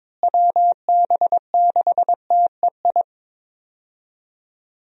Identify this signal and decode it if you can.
Morse code, heard as a keyed tone, about 21 wpm, 700 Hz. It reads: WB6TEI
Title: USA Call Signs - 6 Characters - Repeated 2x - 22wpm